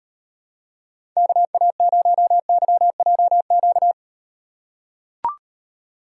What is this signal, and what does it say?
Morse code, heard as a keyed tone, about 38 wpm, 700 Hz. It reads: KA0YJQ